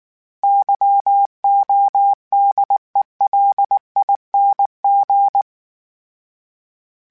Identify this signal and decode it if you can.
Morse code, heard as a keyed tone, about 19 wpm, 800 Hz. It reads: YODELING